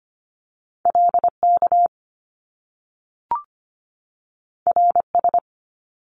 Morse code, heard as a keyed tone, about 25 wpm, 700 Hz. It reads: LK RS
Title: Top Two Letter Combinations in Words 25wpm